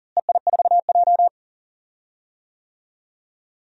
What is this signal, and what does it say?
Morse code, heard as a keyed tone, about 40 wpm, 700 Hz. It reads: EI4J
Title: DXCC-100 Call Signs - 4 Characters - Repeated 3x - 40wpm